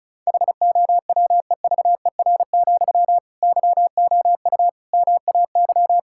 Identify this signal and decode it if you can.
Morse code, heard as a keyed tone, about 34 wpm, 700 Hz. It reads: HOWEVER, YOU MAY
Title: Sentences from Top 1600 Words 35wpm